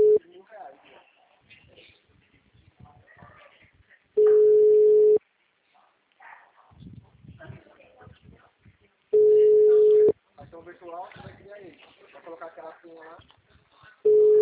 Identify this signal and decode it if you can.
Ringback tone (Europe, 425 Hz), with speech.
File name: out- -2007- - - .90989